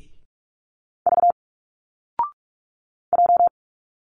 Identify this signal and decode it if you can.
Morse code, heard as a keyed tone, about 46 wpm, 700 Hz. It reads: V J
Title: Single Letters 45wpm